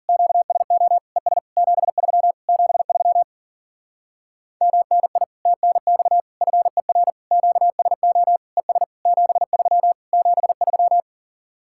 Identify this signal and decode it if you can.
Morse code, heard as a keyed tone, about 52 wpm, 700 Hz. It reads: QSO ES 73 73 MNI TNX FER QSO ES 73 73